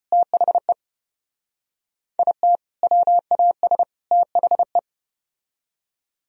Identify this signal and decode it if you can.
Morse code, heard as a keyed tone, about 34 wpm, 700 Hz. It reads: THE IT WAS THE